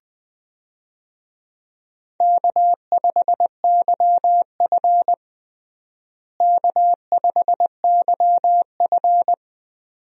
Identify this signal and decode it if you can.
Morse code, heard as a keyed tone, about 20 wpm, 700 Hz. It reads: K5YF K5YF
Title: USA Call Signs - 4 Characters - Repeated 2x - 20wpm